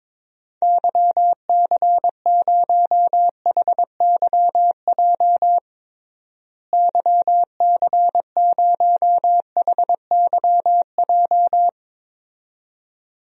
Morse code, heard as a keyed tone, about 22 wpm, 700 Hz. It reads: YC0HYJ YC0HYJ
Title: DXCC-100 Call Signs - 6 Characters - Repeated 2x - 22wpm